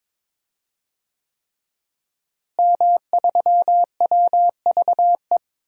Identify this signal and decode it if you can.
Morse code, heard as a keyed tone, about 22 wpm, 700 Hz. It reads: M3WVE
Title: DXCC-100 Call Signs - 5 Characters - Repeated 3x - 22wpm